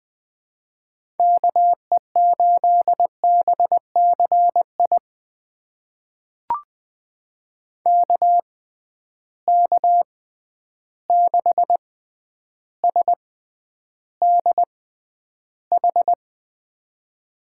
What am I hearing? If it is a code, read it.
Morse code, heard as a keyed tone, about 20 wpm, spaced as for 10 wpm, 700 Hz. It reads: KE8BCI KK6SDH
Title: USA Call Signs - 6 Characters - 6x Character Spacing - 20wpm